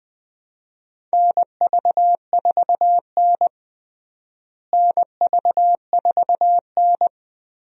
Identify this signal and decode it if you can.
Morse code, heard as a keyed tone, about 20 wpm, 700 Hz. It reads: NV4N NV4N
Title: USA Call Signs - 4 Characters - Repeated 2x - 20wpm